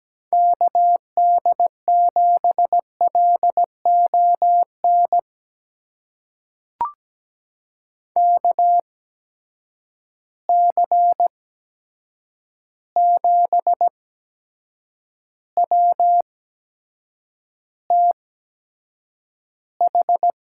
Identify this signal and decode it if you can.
Morse code, heard as a keyed tone, about 17 wpm, spaced as for 7 wpm, 700 Hz. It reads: KD7LON KC7WTH
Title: USA Call Signs - 6 Characters - 8x Character Spacing - 17wpm